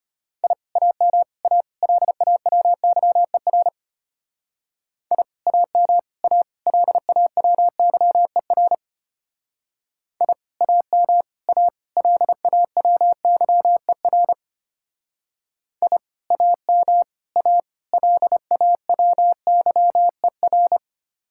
Morse code, heard as a keyed tone, about 32 wpm, 700 Hz. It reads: I AM A LAWYER I AM A LAWYER I AM A LAWYER I AM A LAWYER